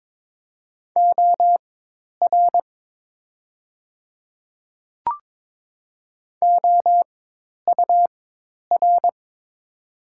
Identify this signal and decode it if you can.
Morse code, heard as a keyed tone, about 22 wpm, spaced as for 10 wpm, 700 Hz. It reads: OR OUR
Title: 4x Character Spacing - Top 200 Words - 22wpm